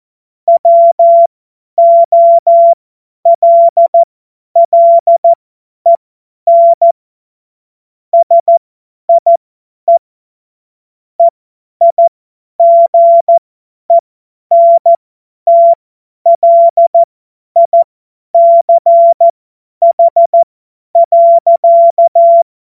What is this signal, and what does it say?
Morse code, heard as a keyed tone, about 14 wpm, spaced as for 10 wpm, 680 Hz. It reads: WOLLEN SIE EIGENTLICH.
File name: Morsekurs / Witze / Witz 15. November 2022 50 F70